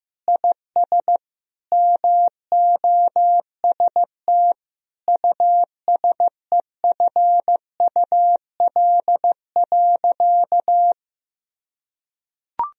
Morse code, heard as a keyed tone, about 15 wpm, 700 Hz. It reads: IS MOST USEFUL.
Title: Sentences from Top 1000 Words - Review Part 4 - 15wpm